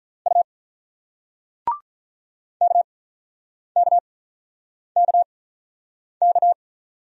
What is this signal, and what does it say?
Morse code, heard as a keyed tone, about 47 wpm, spaced as for 9 wpm, 690 Hz. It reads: U KKKK